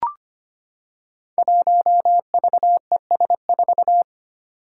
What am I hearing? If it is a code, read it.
Morse code, heard as a keyed tone, about 25 wpm, 700 Hz. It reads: 1VES4
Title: Five Challenging Letter-Number Combinations 25wpm